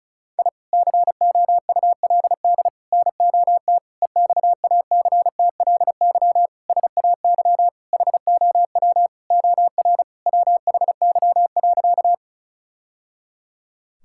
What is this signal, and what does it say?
Morse code, heard as a keyed tone, about 35 wpm, 700 Hz. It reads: I COULD NOT EXACTLY SAY HOW OR WHY.
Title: Sentences from Top 1000 Words - Review Part 6 - 35wpm